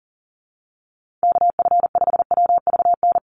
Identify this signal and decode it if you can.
Morse code, heard as a keyed tone, about 40 wpm, 690 Hz. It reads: KF5WVN
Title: USA Call Signs 40wpm